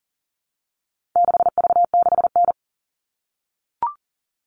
Morse code, heard as a keyed tone, about 38 wpm, 700 Hz. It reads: 6V6N
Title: All Cut Numbers - 4-Digit 40wpm